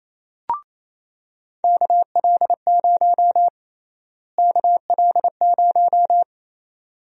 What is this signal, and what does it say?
Morse code, heard as a keyed tone, about 28 wpm, 700 Hz. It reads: KL0 KL0
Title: USA Call Sign Prefixes - Repeated 2x - 28wpm